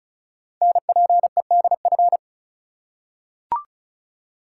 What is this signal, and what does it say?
Morse code, heard as a keyed tone, about 36 wpm, 700 Hz. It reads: NPEDF
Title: Code Groups - 5 Letter:Numbers - 35wpm